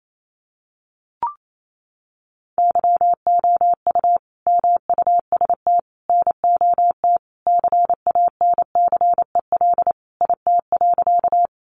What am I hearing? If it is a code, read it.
Morse code, heard as a keyed tone, about 28 wpm, 700 Hz. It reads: YOU MUST NOT CANCEL IT.
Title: Sentences from Top 1200 Words 28wpm